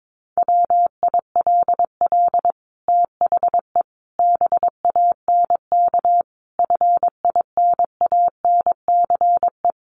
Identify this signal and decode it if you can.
Morse code, heard as a keyed tone, about 22 wpm, 700 Hz. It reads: WILL THE BANK FINANCE
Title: Sentences from Top 1500 Words 22wpm